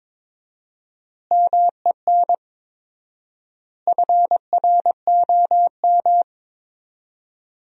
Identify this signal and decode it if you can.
Morse code, heard as a keyed tone, about 21 wpm, 700 Hz. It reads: MEN FROM
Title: 4x Word-Spacing - Sentences from Top 300 Words - 22wpm